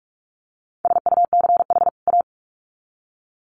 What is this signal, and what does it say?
Morse code, heard as a keyed tone, about 44 wpm, 700 Hz. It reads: SUCH A
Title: Sets of 2 Words 45wpm